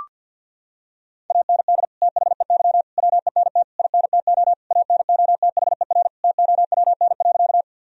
Morse code, heard as a keyed tone, about 52 wpm, 700 Hz. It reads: AND THEY WENT INTO ANOTHER TOWN.
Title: Sentences from Top 1000 Words - Review Part 2 - 50wpm